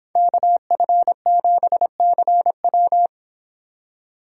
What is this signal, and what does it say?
Morse code, heard as a keyed tone, about 24 wpm, 700 Hz. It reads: KF7CW